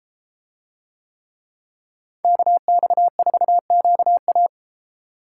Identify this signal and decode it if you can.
Morse code, heard as a keyed tone, about 33 wpm, 700 Hz. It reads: KX4QA